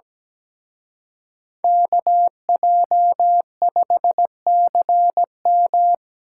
Morse code, heard as a keyed tone, about 21 wpm, 700 Hz. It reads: KJ5CM
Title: Speed Racing - USA 2x2 Call Signs 17wpm